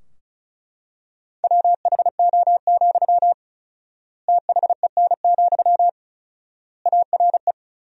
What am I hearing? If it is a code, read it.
Morse code, heard as a keyed tone, about 36 wpm, 700 Hz. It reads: WHO, THEN, ARE
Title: 4x Word-Spacing - Sentences from Top 100 Words - 35wpm